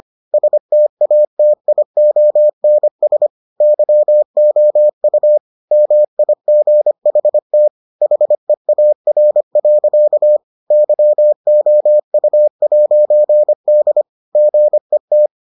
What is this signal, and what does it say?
Morse code, heard as a keyed tone, about 25 wpm, 600 Hz. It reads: STATIONS YOU MIGHT HEAR. YOU'D GET